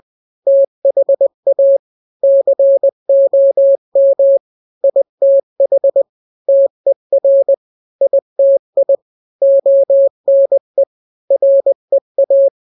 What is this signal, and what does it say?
Morse code, heard as a keyed tone, about 20 wpm, 550 Hz. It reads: THA COM ITH TER ITI ONE REA
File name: n-grams / 3-grams 20 18 WPM